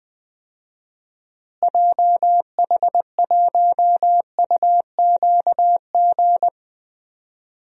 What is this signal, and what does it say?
Morse code, heard as a keyed tone, about 20 wpm, 700 Hz. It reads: JH1UQG
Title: DXCC-100 Call Signs - 6 Characters - Repeated 3x - 20wpm